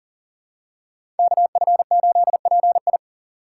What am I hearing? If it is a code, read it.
Morse code, heard as a keyed tone, about 38 wpm, 700 Hz. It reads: KF8PI
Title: USA Call Signs - 5 Characters - 3x Character Spacing - 40wpm